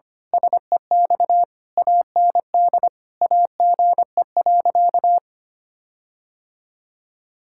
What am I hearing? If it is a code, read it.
Morse code, heard as a keyed tone, about 25 wpm, 700 Hz. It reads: SEX AND AGE.
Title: Sentences from Top 1600 Words 25wpm